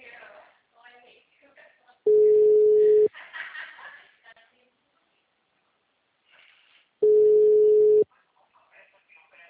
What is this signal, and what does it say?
Ringback tone (Europe, 425 Hz), with speech.